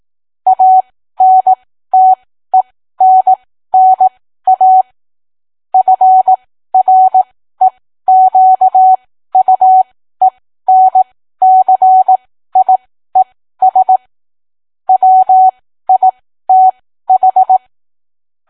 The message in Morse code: ANTENNA FREQUENCIES WITH